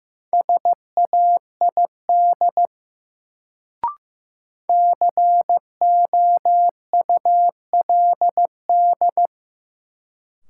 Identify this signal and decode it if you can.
Morse code, heard as a keyed tone, about 16 wpm, 700 Hz. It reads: SAID COULD
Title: Sentences from Top 100 Words - New Word Drill - 15wpm